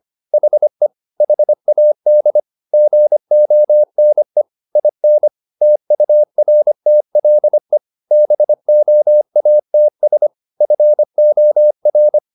Morse code, heard as a keyed tone, about 25 wpm, 600 Hz. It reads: HE HAD GONE IN TURTLE BOATS FOR
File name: the old man and the sea part 1 25wpm